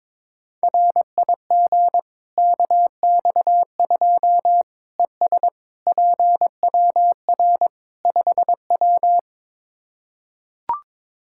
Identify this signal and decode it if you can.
Morse code, heard as a keyed tone, about 22 wpm, 700 Hz. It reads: RIG KX2 ES PWR 5W
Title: K1USN SST QSO Practice 22wpm